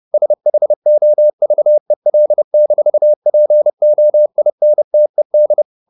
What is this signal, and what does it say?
Morse code, heard as a keyed tone, about 30 wpm, 600 Hz. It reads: SHOVEL-POINTED